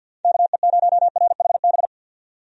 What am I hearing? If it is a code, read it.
Morse code, heard as a keyed tone, about 47 wpm, 700 Hz. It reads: KE0RHB
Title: USA Call Signs - 6 Characters - 50wpm